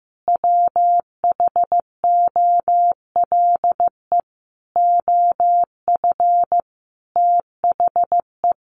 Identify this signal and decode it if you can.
Morse code, heard as a keyed tone, about 15 wpm, 700 Hz. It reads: WHOLE OF THE
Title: Sentences from Top 1400 Words 15wpm